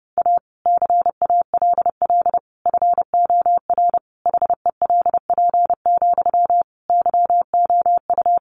Morse code, heard as a keyed tone, about 30 wpm, 700 Hz. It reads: A CALL FOR HELP, YOU
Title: Sentences from Top 300 Words 30wpm